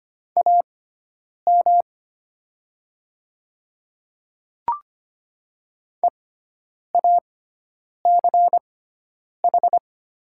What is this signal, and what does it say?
Morse code, heard as a keyed tone, about 25 wpm, spaced as for 9 wpm, 700 Hz. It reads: AM EACH
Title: 6x Character Spacing - Top 500 Words - 25wpm